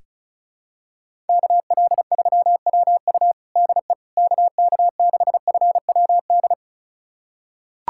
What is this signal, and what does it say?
Morse code, heard as a keyed tone, about 34 wpm, 700 Hz. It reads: KL3WU DE KK6FWD
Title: USA Callsign DE Callsign 35wpm